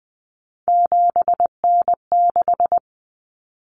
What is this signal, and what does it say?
Morse code, heard as a keyed tone, about 20 wpm, 700 Hz. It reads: 7N6
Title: Common Cut Numbers - 3-Digit 20wpm